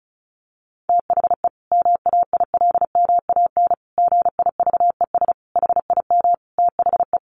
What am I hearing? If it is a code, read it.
Morse code, heard as a keyed tone, about 34 wpm, 700 Hz. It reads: THE MAILMAN GIVES HIM THE
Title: Easy Phrases 35wpm